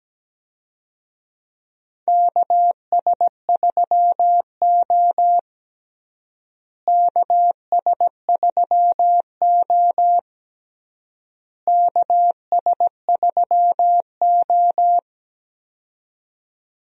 Morse code, heard as a keyed tone, about 17 wpm, 700 Hz. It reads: KS3O KS3O KS3O
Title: USA Call Signs - 4 Characters - Repeated 3x - 17wpm